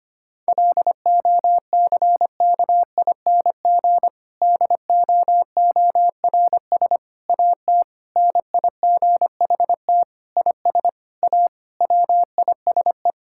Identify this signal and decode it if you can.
Morse code, heard as a keyed tone, about 25 wpm, 700 Hz. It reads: LOCKING DOORS AT NIGHT IS A WISE
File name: practice / 25wpm / Difficult Sentences with Hints - 25wpm